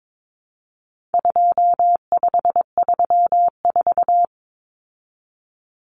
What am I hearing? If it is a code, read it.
Morse code, heard as a keyed tone, about 22 wpm, 700 Hz. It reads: 2534
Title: Numbers - 4-Digit 22wpm